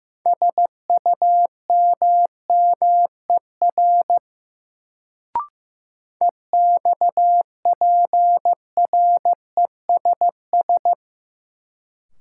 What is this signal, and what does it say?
Morse code, heard as a keyed tone, about 15 wpm, 700 Hz. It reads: SUMMER EXPRESS
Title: Sentences from Top 1100 Words - New Word Drill Alt - 15wpm